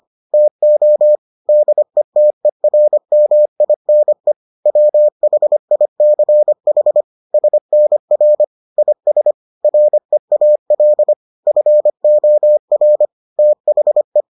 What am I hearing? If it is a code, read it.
Morse code, heard as a keyed tone, about 25 wpm, 600 Hz. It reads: TO DETERMINE WHICH SNR IS REAL FOR THE